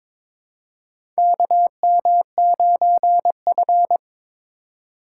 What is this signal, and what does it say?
Morse code, heard as a keyed tone, about 22 wpm, 700 Hz. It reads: KM9F
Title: USA Call Signs - 4x Character Spacing - 22wpm